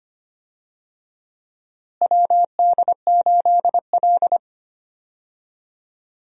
Morse code, heard as a keyed tone, about 25 wpm, 700 Hz. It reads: WD8L
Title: USA Call Signs - 4 Characters - Repeated 2x - 25wpm